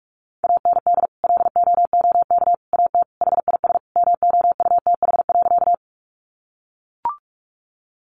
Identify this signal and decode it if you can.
Morse code, heard as a keyed tone, about 44 wpm, 700 Hz. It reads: AND LOOK AT HIS MOUTH.
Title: Sentences from Top 1500 Words 45wpm